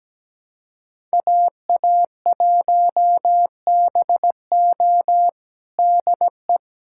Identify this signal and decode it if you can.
Morse code, heard as a keyed tone, about 17 wpm, 700 Hz. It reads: AA1BO DE
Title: USA Callsign DE Callsign - 5 Characters - 17wpm